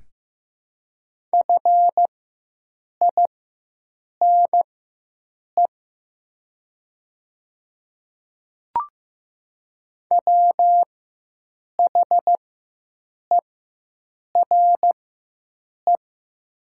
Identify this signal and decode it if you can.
Morse code, heard as a keyed tone, about 15 wpm, spaced as for 7 wpm, 700 Hz. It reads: FINE WHERE
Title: 4x Character Spacing - Top 500 Words - 15wpm